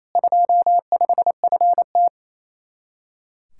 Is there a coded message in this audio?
Yes: Morse code from 0.1 to 2.1 s, heard as a keyed tone, about 37 wpm, 700 Hz. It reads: 25FT